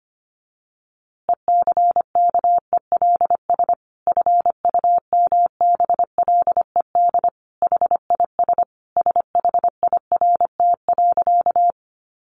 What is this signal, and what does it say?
Morse code, heard as a keyed tone, about 25 wpm, 700 Hz. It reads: ECKELS FUMBLED HIS SHIRT.
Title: A Sound Of Thunder 25wpm